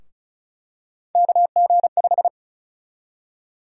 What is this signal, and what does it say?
Morse code, heard as a keyed tone, about 36 wpm, 700 Hz. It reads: KG5
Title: USA Call Sign Prefixes - Repeated 2x - 35wpm